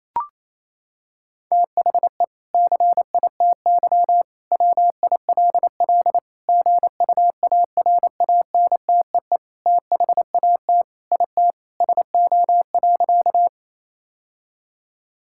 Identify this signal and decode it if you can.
Morse code, heard as a keyed tone, about 28 wpm, 700 Hz. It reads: THE CITY WILL GUARANTEE THAT IT SO.
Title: Sentences from Top 1600 Words 28wpm